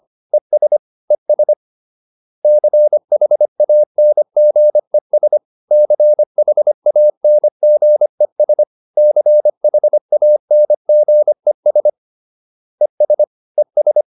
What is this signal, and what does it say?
Morse code, heard as a keyed tone, about 25 wpm, 600 Hz. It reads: ES ES CHANGES CHANGES CHANGES ES ES